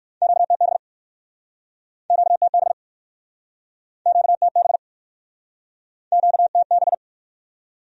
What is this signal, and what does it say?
Morse code, heard as a keyed tone, about 59 wpm, 700 Hz. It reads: QTB QTB QTB QTB